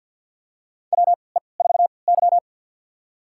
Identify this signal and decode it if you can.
Morse code, heard as a keyed tone, about 53 wpm, spaced as for 29 wpm, 700 Hz. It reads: WE4Y
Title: USA Call Signs - 4 Characters - 3x Character Spacing - 50wpm